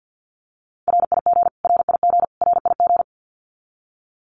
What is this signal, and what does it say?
Morse code, heard as a keyed tone, about 50 wpm, 700 Hz. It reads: RIG RIG RIG
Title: Abbreviations Repeated 3x - 50wpm